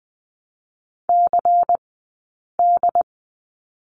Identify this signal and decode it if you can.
Morse code, heard as a keyed tone, about 20 wpm, 700 Hz. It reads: C D
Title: 092 Lesson 16d - Letter c - Two Characters 20wpm